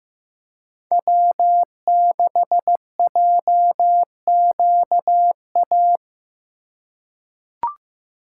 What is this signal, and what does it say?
Morse code, heard as a keyed tone, about 15 wpm, 700 Hz. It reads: W6JQA
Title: USA Call Signs - 5 Characters - 4x Character Spacing - 15wpm